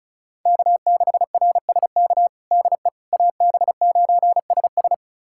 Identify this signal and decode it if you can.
Morse code, heard as a keyed tone, about 34 wpm, 700 Hz. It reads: K6RSK DE AB9SS
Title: USA Callsign DE Callsign - 5 Characters - 35wpm